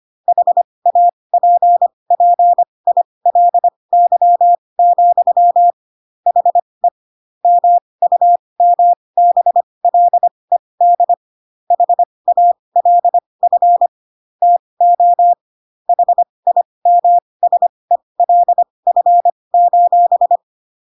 Morse code, heard as a keyed tone, about 25 wpm, spaced as for 20 wpm, 700 Hz. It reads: HAPPILY, HE MUMBLED HALF TO HIMSELF:
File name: Pinocchio / Pinocchio ch1 15 25 ik1hge